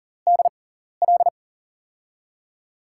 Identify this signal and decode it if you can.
Morse code, heard as a keyed tone, about 40 wpm, spaced as for 14 wpm, 700 Hz. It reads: DL
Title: USA Call Signs - 6 Characters - 6x Character Spacing - 40wpm